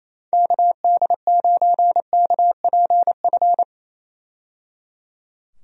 Morse code, heard as a keyed tone, about 28 wpm, 700 Hz. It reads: KD9KPF
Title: USA Call Signs - 6 Characters - 28wpm